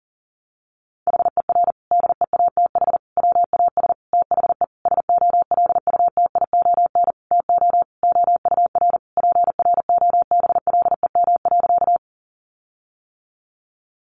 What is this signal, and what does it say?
Morse code, heard as a keyed tone, about 37 wpm, 700 Hz. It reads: HER DEATH WAS THE SOLUTION TO OUR PROBLEM.
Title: Sentences from Top 1300 Words 40wpm